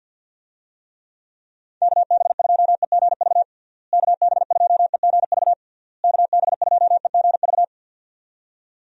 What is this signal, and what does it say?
Morse code, heard as a keyed tone, about 51 wpm, 700 Hz. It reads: KB2EGV KB2EGV KB2EGV
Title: USA Call Signs - 6 Characters - Repeated 3x - 50wpm